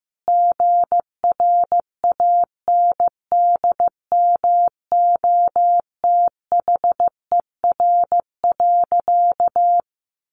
Morse code, heard as a keyed tone, about 15 wpm, 700 Hz. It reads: GRANDMOTHER.
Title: Sentences from Top 1200 Words 15wpm